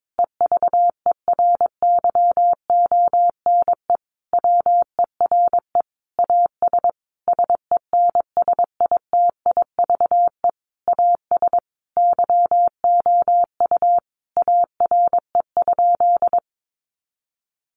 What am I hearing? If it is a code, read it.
Morse code, heard as a keyed tone, about 22 wpm, 700 Hz. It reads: EVERYONE WERE AS SENSITIVE AS YOU ARE?
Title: Sentences from Top 1400 Words 22wpm